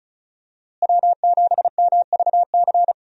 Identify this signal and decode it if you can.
Morse code, heard as a keyed tone, about 34 wpm, 700 Hz. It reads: W7MVC
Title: USA Call Signs - 5 Characters - Repeated 2x - 35wpm